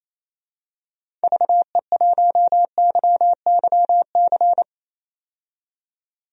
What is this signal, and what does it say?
Morse code, heard as a keyed tone, about 28 wpm, 700 Hz. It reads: VE1YYC
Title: DXCC-100 Call Signs - 6 Characters - 28wpm